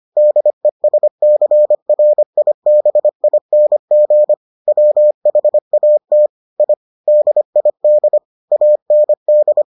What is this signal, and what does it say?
Morse code, heard as a keyed tone, about 25 wpm, 600 Hz. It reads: DESCRIBING WHAT I DID AND